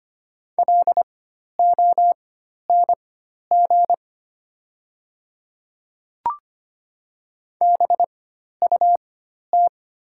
Morse code, heard as a keyed tone, about 25 wpm, spaced as for 12 wpm, 690 Hz. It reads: LONG BUT
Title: 4x Character Spacing - Top 100 Words - 25wpm